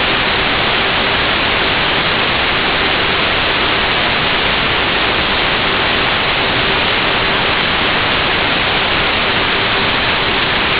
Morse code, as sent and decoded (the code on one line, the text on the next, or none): none